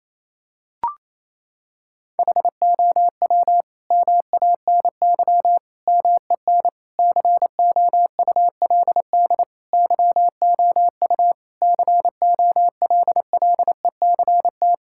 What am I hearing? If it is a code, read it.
Morse code, heard as a keyed tone, about 28 wpm, 700 Hz. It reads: HOW MANY MEN COULD YOU COLLECT
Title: Sentences from Top 1600 Words 28wpm